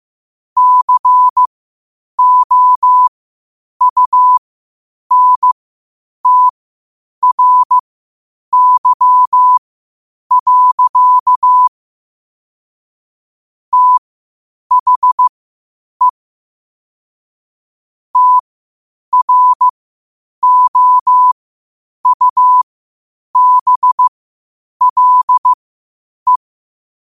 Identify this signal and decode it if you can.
Morse code, heard as a keyed tone, about 15 wpm, spaced as for 8 wpm, 1000 Hz. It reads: COUNTRY. THE TROUBLE
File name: qotdfiles / 8wpm